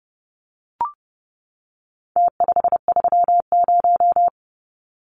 Morse code, heard as a keyed tone, about 30 wpm, 700 Hz. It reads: T530
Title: Numbers - Contest Style - 4-Digit 30wpm